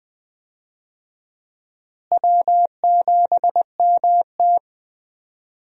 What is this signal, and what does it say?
Morse code, heard as a keyed tone, about 21 wpm, 700 Hz. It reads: W7MT
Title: USA Call Signs - 4 Characters - Repeated 2x - 20wpm